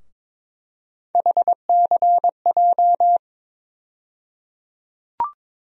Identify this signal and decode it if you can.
Morse code, heard as a keyed tone, about 22 wpm, 700 Hz. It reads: HCJ
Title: USA Call Sign Postfixes - 3 Letters 22wpm